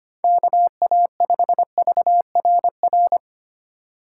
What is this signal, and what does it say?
Morse code, heard as a keyed tone, about 25 wpm, 700 Hz. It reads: KA5VRR
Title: USA Call Signs - 6 Characters - 2x Character Spacing - 25wpm